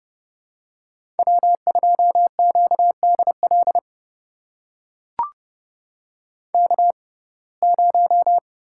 Morse code, heard as a keyed tone, about 31 wpm, spaced as for 22 wpm, 700 Hz. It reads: W2QDL K0
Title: USA Call Signs - 5 Characters - 6x Character Spacing - 30wpm